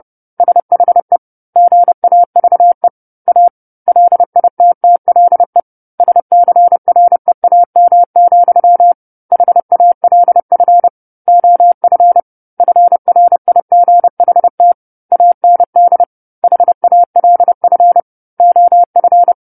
Morse code, heard as a keyed tone, about 30 wpm, 700 Hz. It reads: SHE GAVE A LITTLE SCREAM, HALF OF FRIGHT AND HALF OF
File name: 10-Chapter-10 30 WPM